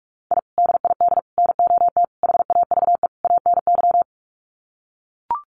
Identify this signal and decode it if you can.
Morse code, heard as a keyed tone, about 44 wpm, 700 Hz. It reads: I DID NOT HAVE ANY
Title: Sets of 5 Words Encore 45wpm